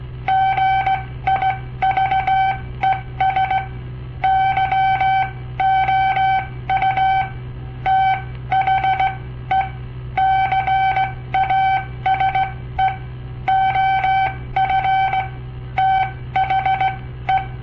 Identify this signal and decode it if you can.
Morse code, heard as a keyed tone, about 17 wpm, spaced as for 14 wpm, 770 Hz. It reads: GIVES YOU THE CASE OF THE
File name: papertapes / local / Code Practice-01